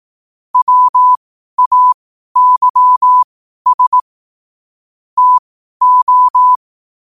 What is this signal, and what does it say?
Morse code, heard as a keyed tone, about 18 wpm, spaced as for 12 wpm, 1000 Hz. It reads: WAYS TO
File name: qotdfiles / 12wpm